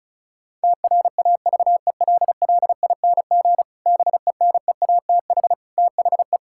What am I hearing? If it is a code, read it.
Morse code, heard as a keyed tone, about 36 wpm, 700 Hz. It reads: TRAVELLING BENEATH THE
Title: Pangram Challenge v2 - 35wpm